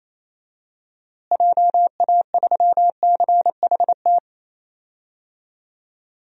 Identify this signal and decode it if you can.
Morse code, heard as a keyed tone, about 28 wpm, 700 Hz. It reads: JA3CHT
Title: DXCC-100 Call Signs - 6 Characters - Repeated 3x - 28wpm